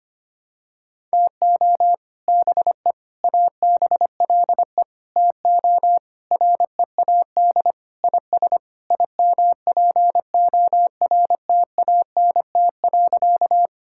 Morse code, heard as a keyed tone, about 25 wpm, 700 Hz. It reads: TO BE ABLE TO READ IS IMPORTANT.
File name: practice / 25wpm / Sentences with 3-Word Phrases - 25wpm